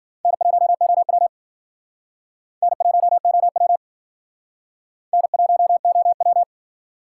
Morse code, heard as a keyed tone, about 51 wpm, 700 Hz. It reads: N1OW N1OW N1OW